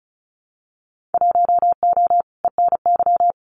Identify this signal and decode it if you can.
Morse code, heard as a keyed tone, about 34 wpm, 700 Hz. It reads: 1O ENY
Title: Winter Field Day Exchanges 35wpm